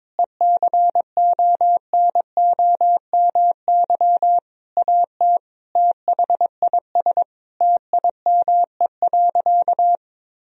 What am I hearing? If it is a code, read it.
Morse code, heard as a keyed tone, about 22 wpm, 700 Hz. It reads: ECONOMY AT THIS TIME.
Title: Sentences from Top 1000 Words - Review Part 3 - 22wpm